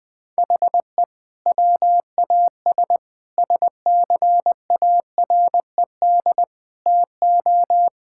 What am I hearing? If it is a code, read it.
Morse code, heard as a keyed tone, about 21 wpm, 700 Hz. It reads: HE WAS SCARED TO
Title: Sentences from Top 1000 Words - Review Part 6 - 20wpm